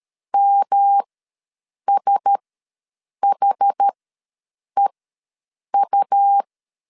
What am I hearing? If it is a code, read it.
Morse code, heard as a keyed tone, about 13 wpm, spaced as for 7 wpm, 800 Hz. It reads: MSHEU
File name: LEZIONI / 5 LBVC / 5 LBVC ws 12wpm 3159